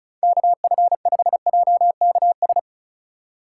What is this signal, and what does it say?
Morse code, heard as a keyed tone, about 34 wpm, 700 Hz. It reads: KF5JKS
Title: USA Call Signs - 4x Character Spacing - 35wpm